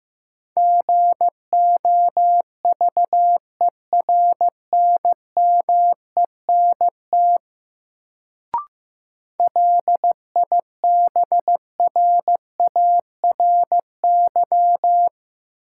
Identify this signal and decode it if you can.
Morse code, heard as a keyed tone, about 15 wpm, 700 Hz. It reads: GOVERNMENT LIBRARY
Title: Sentences from Top 600 Words - New Word Drill - 15wpm